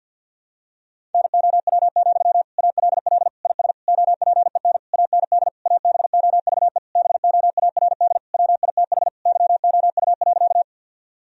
Morse code, heard as a keyed tone, about 50 wpm, 700 Hz. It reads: NOW, ALL IS OPEN AND ABOVE BOARD WITH YOU.
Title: Sentences from Top 1000 Words - Review Part 4 - 50wpm